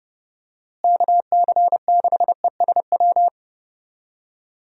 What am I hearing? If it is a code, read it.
Morse code, heard as a keyed tone, about 32 wpm, 700 Hz. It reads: KC6ESW
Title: USA Call Signs - 6 Characters - Repeated 2x - 30wpm